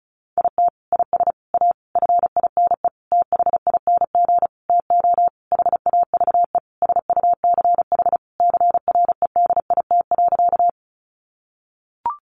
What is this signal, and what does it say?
Morse code, heard as a keyed tone, about 34 wpm, 700 Hz. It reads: IT IS A FINE THING TO HAVE SUCH CREDIT.
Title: Sentences from Top 1000 Words 35wpm